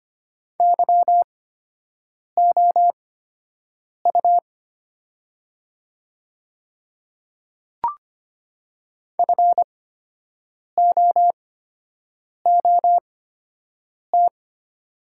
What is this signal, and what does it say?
Morse code, heard as a keyed tone, about 25 wpm, spaced as for 7 wpm, 700 Hz. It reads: YOU FOOT
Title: 8x Character Spacing - Top 500 Words - 25wpm